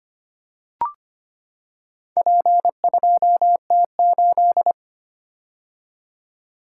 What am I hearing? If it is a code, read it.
Morse code, heard as a keyed tone, about 25 wpm, 700 Hz. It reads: P2T8
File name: practice / 25wpm / Four Letter-Number Combinations - 25wpm